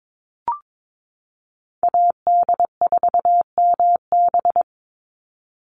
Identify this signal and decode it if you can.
Morse code, heard as a keyed tone, about 22 wpm, 700 Hz. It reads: AD4MB
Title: USA Call Signs 22wpm